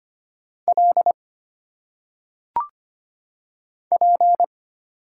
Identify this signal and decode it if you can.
Morse code, heard as a keyed tone, about 25 wpm, 700 Hz. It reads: L P
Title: Single Letters - Phonetic Pronunciation 25wpm